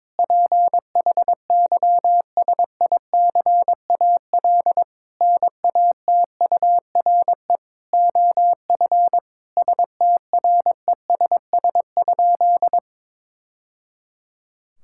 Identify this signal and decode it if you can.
Morse code, heard as a keyed tone, about 22 wpm, 700 Hz. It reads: PHYSICAL NATURE OF STRESS?
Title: Sentences from Top 1000 Words - Review Part 6 - 22wpm